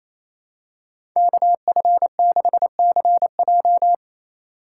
Morse code, heard as a keyed tone, about 28 wpm, 700 Hz. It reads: KF6CJ
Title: USA Call Signs - 3x Character Spacing - 28wpm